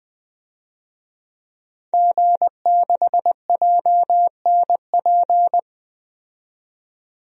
Morse code, heard as a keyed tone, about 20 wpm, 700 Hz. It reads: G6JNP
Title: DXCC-100 Call Signs - 5 Characters - 20wpm